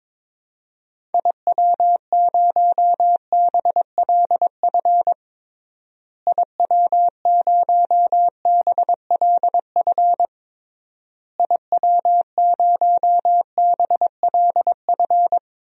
Morse code, heard as a keyed tone, about 22 wpm, 700 Hz. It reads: IW0BLF IW0BLF IW0BLF
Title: DXCC-100 Call Signs - 6 Characters - Repeated 3x - 22wpm